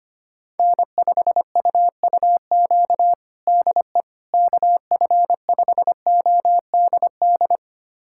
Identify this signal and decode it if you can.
Morse code, heard as a keyed tone, about 25 wpm, 700 Hz. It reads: N5UUQ DE KF5ODD
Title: USA Callsign DE Callsign 25wpm